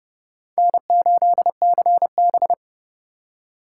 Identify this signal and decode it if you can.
Morse code, heard as a keyed tone, about 36 wpm, 700 Hz. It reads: N8CB
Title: Speed Racing - USA 1x2 Call Signs 30wpm